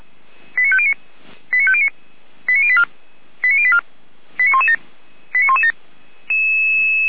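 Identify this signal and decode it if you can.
Five-tone selective call (ZVEI), heard as quick runs of five tones, tones of about 70 ms. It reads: 89509 89509 89084 89084 89117 89117
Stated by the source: ZVEI selective call received on an SDR by the Feuerwehr Orlamünde fire brigade.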